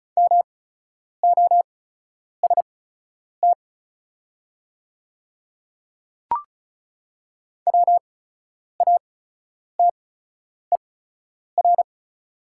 Morse code, heard as a keyed tone, about 36 wpm, spaced as for 10 wpm, 700 Hz. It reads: MOST WATER
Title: 8x Character Spacing - Top 500 Words - 35wpm